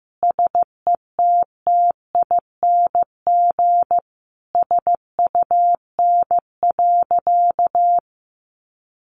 Morse code, heard as a keyed tone, about 15 wpm, 700 Hz. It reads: SETTING SUN.
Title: Sentences from Top 1300 Words 15wpm